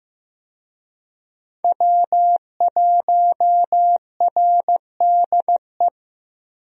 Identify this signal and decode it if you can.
Morse code, heard as a keyed tone, about 15 wpm, 700 Hz. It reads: W1RDE
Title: USA Call Signs - 5 Characters - Repeated 2x - 15wpm